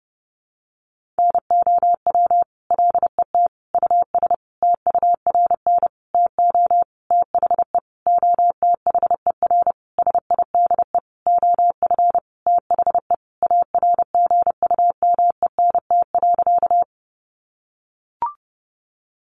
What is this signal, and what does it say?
Morse code, heard as a keyed tone, about 30 wpm, 700 Hz. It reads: NOW LET US TURN TO THE OTHER SIDE OF THE ARGUMENT.
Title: Sentences from Top 1400 Words 30wpm